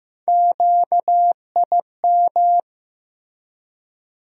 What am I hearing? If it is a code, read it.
Morse code, heard as a keyed tone, about 15 wpm, 700 Hz. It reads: QIM
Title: USA Call Sign Postfixes - 3 Letters 15wpm